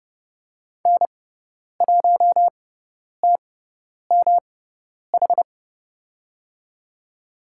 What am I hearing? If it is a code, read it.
Morse code, heard as a keyed tone, about 31 wpm, spaced as for 10 wpm, 700 Hz. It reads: N1TMH
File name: Lesson 18g - Letter m - Call Signs 30wpm